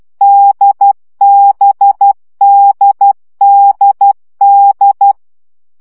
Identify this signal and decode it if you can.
Morse code, heard as a keyed tone, about 12 wpm, 800 Hz. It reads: DBDDD
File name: CW gyakorlas d b x 10 baud